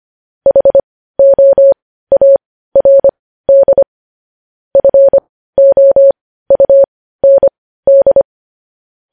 Morse code, heard as a keyed tone, about 25 wpm, spaced as for 15 wpm, 550 Hz. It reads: HOARD FOUND
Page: news-headlines-in-morse-code-at-15-wpm